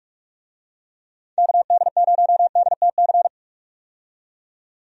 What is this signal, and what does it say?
Morse code, heard as a keyed tone, about 46 wpm, 700 Hz. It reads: KD0DTC
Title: USA Call Signs - 6 Characters - 45wpm